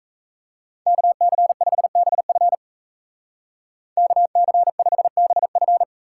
Morse code, heard as a keyed tone, about 38 wpm, 700 Hz. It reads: KC5BF KC5BF